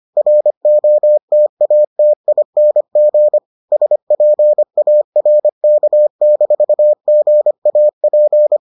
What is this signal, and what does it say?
Morse code, heard as a keyed tone, about 25 wpm, 600 Hz. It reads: ROTATING SPARK-GAP